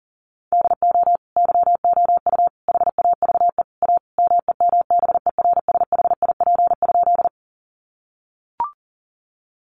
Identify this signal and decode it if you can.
Morse code, heard as a keyed tone, about 40 wpm, 700 Hz. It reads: DO YOU HAVE A MEMBERSHIP?
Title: Sentences from Top 1400 Words 40wpm